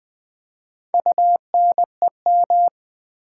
Morse code, heard as a keyed tone, about 20 wpm, 700 Hz. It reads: UNEM
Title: Build A Word - 4 Letters At A Time - 12-Letter Words - 20wpm